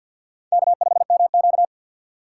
Morse code, heard as a keyed tone, about 51 wpm, 700 Hz. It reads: K5MQ
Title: USA Call Signs - 3x Character Spacing - 50wpm